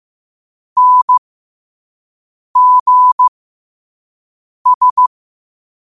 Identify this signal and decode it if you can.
Morse code, heard as a keyed tone, about 15 wpm, spaced as for 5 wpm, 1000 Hz. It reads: NGS
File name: qotdfiles / 5wpm